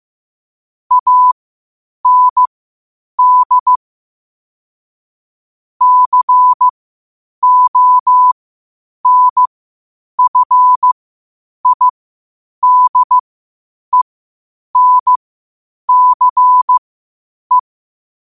Morse code, heard as a keyed tone, about 15 wpm, spaced as for 8 wpm, 1000 Hz. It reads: AND CONFIDENCE
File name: qotdfiles / 8wpm